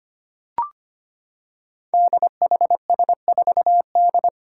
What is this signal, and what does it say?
Morse code, heard as a keyed tone, about 25 wpm, 700 Hz. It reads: DHS4D
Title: Five Challenging Letter-Number Combinations 25wpm